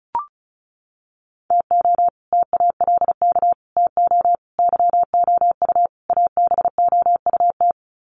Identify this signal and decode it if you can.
Morse code, heard as a keyed tone, about 34 wpm, 700 Hz. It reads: TO TALK TO YOU ABOUT
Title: Sets of 5 Words Encore 35wpm